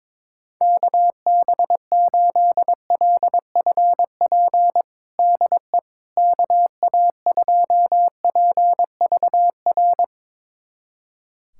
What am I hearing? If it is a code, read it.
Morse code, heard as a keyed tone, about 22 wpm, 700 Hz. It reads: KB8LFP DE KA2PVR
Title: USA Callsign DE Callsign - 6 Characters - 22wpm